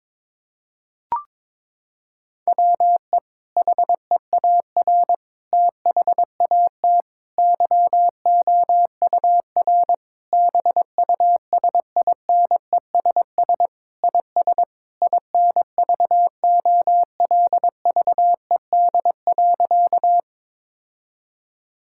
Morse code, heard as a keyed tone, about 22 wpm, 700 Hz. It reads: WE HEAR THAT YOUR BUSINESS IS INVOLVED.
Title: Sentences from Top 1000 Words - Review Part 5 - 22wpm